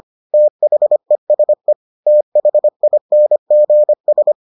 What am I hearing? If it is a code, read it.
Morse code, heard as a keyed tone, about 25 wpm, 600 Hz. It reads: THESE THINGS